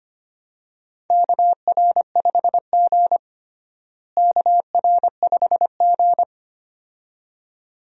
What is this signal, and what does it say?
Morse code, heard as a keyed tone, about 25 wpm, 700 Hz. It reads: KR5G KR5G
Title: USA Call Signs - 4 Characters - Repeated 2x - 25wpm